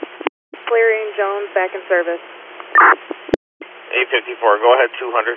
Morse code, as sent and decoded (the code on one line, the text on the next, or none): none